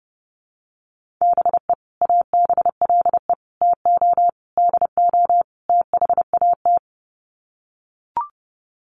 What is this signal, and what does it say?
Morse code, heard as a keyed tone, about 28 wpm, 700 Hz. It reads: BE ABLE TO DO THAT
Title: Sets of 5 Words Encore2 30wpm